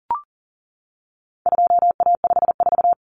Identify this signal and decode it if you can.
Morse code, heard as a keyed tone, about 40 wpm, 700 Hz. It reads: 2A54
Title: Common Cut Numbers - 4-Digit 40wpm